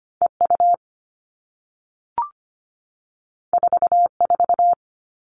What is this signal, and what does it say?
Morse code, heard as a keyed tone, about 25 wpm, 700 Hz. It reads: EU 44
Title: All Cut Numbers - 2-Digit 25wpm